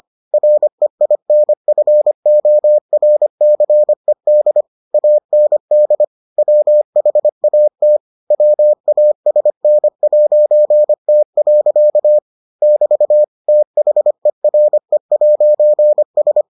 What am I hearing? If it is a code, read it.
Morse code, heard as a keyed tone, about 25 wpm, 600 Hz. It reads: REINFORCED AND WHAT WASN'T. <BT> THERE'S